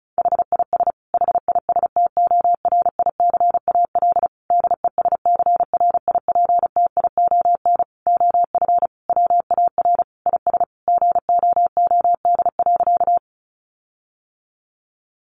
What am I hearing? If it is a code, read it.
Morse code, heard as a keyed tone, about 34 wpm, 700 Hz. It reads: HIS HISTORICAL DESCRIPTION OF WAR IS GOOD.
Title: Sentences from Top 1300 Words 35wpm